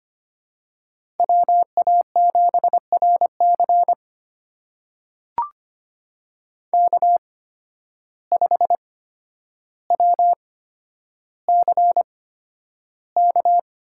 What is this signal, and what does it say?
Morse code, heard as a keyed tone, about 25 wpm, spaced as for 11 wpm, 700 Hz. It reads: WA7RC K5WCK
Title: USA Call Signs - 5 Characters - 8x Character Spacing - 25wpm